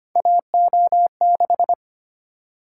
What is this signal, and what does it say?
Morse code, heard as a keyed tone, about 25 wpm, 700 Hz. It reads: AO6
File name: Three Letter-Number Combinations - 25wpm